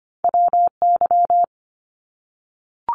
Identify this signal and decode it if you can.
Morse code, heard as a keyed tone, about 25 wpm, 700 Hz. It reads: WY
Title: ARRL Sections 25wpm